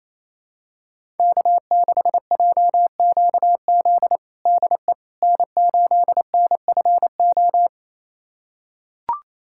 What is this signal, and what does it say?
Morse code, heard as a keyed tone, about 27 wpm, 700 Hz. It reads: K6JQZ DE N8NFO
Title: USA Callsign DE Callsign - 5 Characters - 28wpm